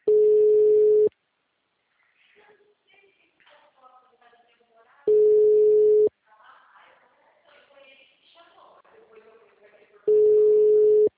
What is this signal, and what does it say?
Ringback tone (Europe, 425 Hz), with speech.